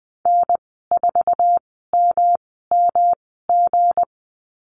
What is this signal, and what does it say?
Morse code, heard as a keyed tone, about 20 wpm, spaced as for 14 wpm, 700 Hz. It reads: N4MMG
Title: 219 Lesson 35g - Number 0 - Call Signs 20wpm